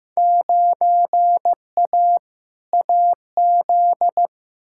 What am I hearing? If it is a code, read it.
Morse code, heard as a keyed tone, about 15 wpm, 700 Hz. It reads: 9A AZ
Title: Field Day Exchanges - 15wpm